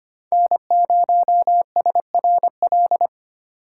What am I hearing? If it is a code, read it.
Morse code, heard as a keyed tone, about 25 wpm, 700 Hz. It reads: N0SRL
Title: USA Call Signs - 5 Characters - 4x Character Spacing - 25wpm